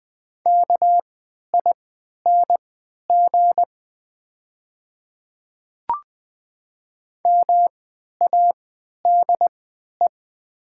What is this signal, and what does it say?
Morse code, heard as a keyed tone, about 20 wpm, spaced as for 11 wpm, 700 Hz. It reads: KING MADE
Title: 3x Character Spacing - Top 500 Words - 20wpm